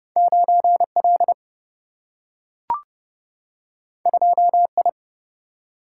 Morse code, heard as a keyed tone, about 30 wpm, 700 Hz. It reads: 9 L 2 I
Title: Single-Letter Single-Number 30wpm